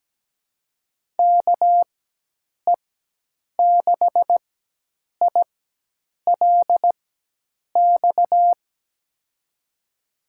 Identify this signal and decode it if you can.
Morse code, heard as a keyed tone, about 17 wpm, spaced as for 8 wpm, 700 Hz. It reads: KE6ILX
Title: USA Call Signs - 6 Characters - 4x Character Spacing - 17wpm